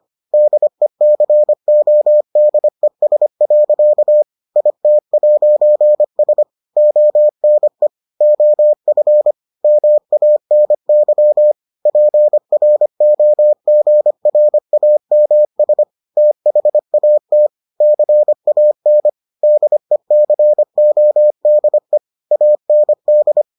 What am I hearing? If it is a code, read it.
Morse code, heard as a keyed tone, about 25 wpm, 600 Hz. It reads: DECODES. IT'S ONE OF MANY PROGRAMS THAT CAN DECODE AND